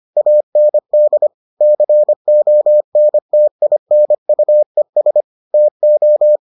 Morse code, heard as a keyed tone, about 25 wpm, 600 Hz. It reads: AND CONTINUES TO